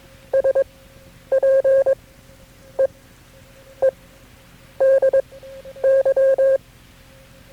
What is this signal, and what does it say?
Morse code, heard as a keyed tone, about 22 wpm, spaced as for 9 wpm, 540 Hz. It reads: SPEEDY
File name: KB6MT-tape3-side1